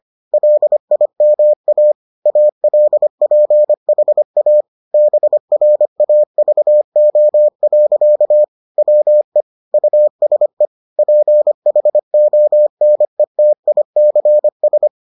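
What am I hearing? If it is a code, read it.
Morse code, heard as a keyed tone, about 25 wpm, 600 Hz. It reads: LIMA ALPHA BRAVO. WE USE PHONETICS
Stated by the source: Morse edition of the Foundations of Amateur Radio episodes.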